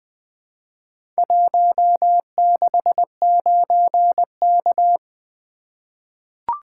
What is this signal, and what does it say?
Morse code, heard as a keyed tone, about 20 wpm, 700 Hz. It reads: 169K
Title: Four Letter-Number Combinations - 20wpm